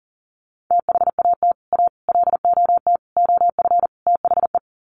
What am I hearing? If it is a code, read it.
Morse code, heard as a keyed tone, about 38 wpm, 700 Hz. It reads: THAT A LOT OF THE
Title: Sets of 5 Words Encore2 40wpm